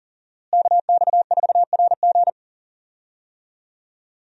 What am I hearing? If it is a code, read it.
Morse code, heard as a keyed tone, about 40 wpm, 700 Hz. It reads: KX4RG
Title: USA Call Signs - 5 Characters - Repeated 2x - 40wpm